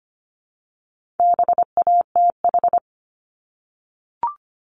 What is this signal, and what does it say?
Morse code, heard as a keyed tone, about 25 wpm, 700 Hz. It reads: BATH
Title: Multi-letter Phonemes Repeated 3x 25wpm